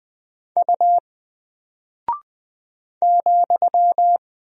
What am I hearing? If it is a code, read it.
Morse code, heard as a keyed tone, about 20 wpm, 700 Hz. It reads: U ,
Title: Single Character (Letter:Number:Punctuation:Prosign) Alt 20wpm